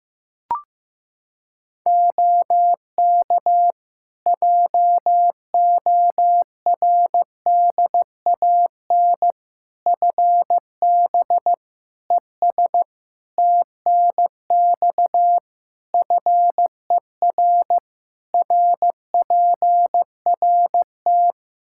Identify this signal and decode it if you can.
Morse code, heard as a keyed tone, about 15 wpm, 700 Hz. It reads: OK JORDAN FB ES TNX FER RPRT
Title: Practice QSO - Exchange 2 - Info - 15wpm